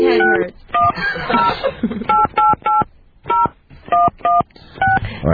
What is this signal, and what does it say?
DTMF, heard as telephone touch tones, tones of about 180 ms, with speech. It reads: 6108880116